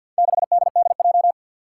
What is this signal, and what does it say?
Morse code, heard as a keyed tone, about 52 wpm, 700 Hz. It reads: 6DDJ
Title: Four Challenging Letter-Number Combinations 50wpm